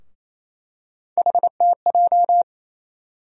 Morse code, heard as a keyed tone, about 28 wpm, 700 Hz. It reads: HTJ
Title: USA Call Sign Postfixes - 3 Letters 28wpm